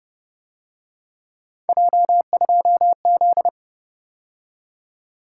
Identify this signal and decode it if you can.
Morse code, heard as a keyed tone, about 30 wpm, 700 Hz. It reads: J2Z
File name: Speed-Racing - Three Letter-Number Combinations - 25wpm